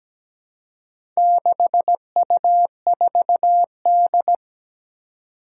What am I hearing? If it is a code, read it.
Morse code, heard as a keyed tone, about 17 wpm, 700 Hz. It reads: 6U4D
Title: Four Challenging Letter-Number Combinations 17wpm